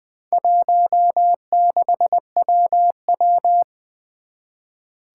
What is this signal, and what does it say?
Morse code, heard as a keyed tone, about 20 wpm, 700 Hz. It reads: 16WW
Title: Four Challenging Letter-Number Combinations 20wpm